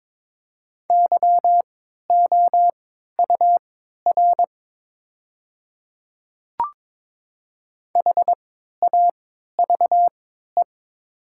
Morse code, heard as a keyed tone, about 22 wpm, spaced as for 12 wpm, 700 Hz. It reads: YOUR HAVE
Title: 3x Character Spacing - Top 100 Words - 22wpm